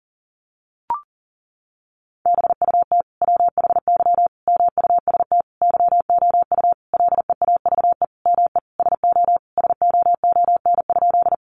Morse code, heard as a keyed tone, about 40 wpm, 700 Hz. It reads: BUT WHY MUST YOU LEAVE ME SO SOON?
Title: Sentences from Top 400 Words 40wpm